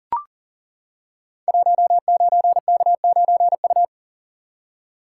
Morse code, heard as a keyed tone, about 41 wpm, 700 Hz. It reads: 19K9U
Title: Code Groups - 5 Letter:Numbers - 40wpm